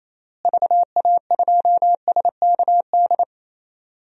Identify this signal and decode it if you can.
Morse code, heard as a keyed tone, about 27 wpm, 700 Hz. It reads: VA2SKD
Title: DXCC-100 Call Signs - 6 Characters - Repeated 2x - 28wpm